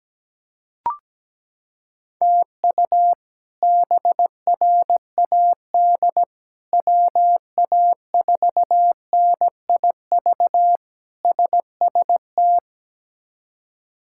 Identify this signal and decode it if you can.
Morse code, heard as a keyed tone, about 17 wpm, 700 Hz. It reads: TU BRAD WA4NIV SST
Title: K1USN SST QSO Practice - 17wpm